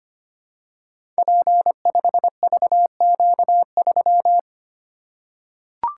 Morse code, heard as a keyed tone, about 25 wpm, 700 Hz. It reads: P5VQ3
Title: Code Groups - 5 Letter:Numbers - 25wpm